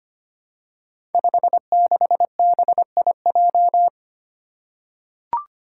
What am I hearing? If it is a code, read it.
Morse code, heard as a keyed tone, about 25 wpm, 700 Hz. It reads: 56BIJ
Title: Five Challenging Letter-Number Combinations 25wpm